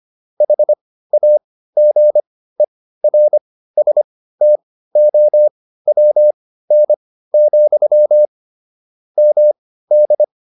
Morse code, heard as a keyed tone, about 25 wpm, spaced as for 15 wpm, 600 Hz. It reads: HAGERSTOWN, MD
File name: QSO307 15